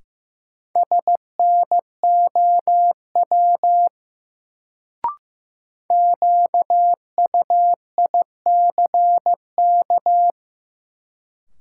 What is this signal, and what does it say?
Morse code, heard as a keyed tone, about 15 wpm, 700 Hz. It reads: SNOW QUICK
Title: Sentences from Top 500 Words - New Word Drill - 15wpm